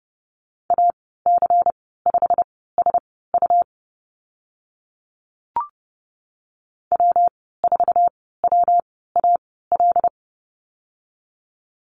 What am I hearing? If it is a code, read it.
Morse code, heard as a keyed tone, about 30 wpm, spaced as for 17 wpm, 700 Hz. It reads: AC5SU W4WAL
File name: Lesson 19g - Letter w - Call Signs 30wpm